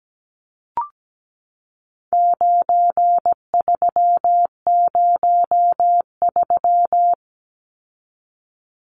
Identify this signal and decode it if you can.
Morse code, heard as a keyed tone, about 17 wpm, 700 Hz. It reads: 9303
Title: Numbers - 4-Digit 17wpm